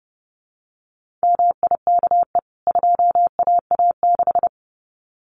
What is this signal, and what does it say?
Morse code, heard as a keyed tone, about 30 wpm, 700 Hz. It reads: MIKE 2AA6
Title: CWT Exchange 30wpm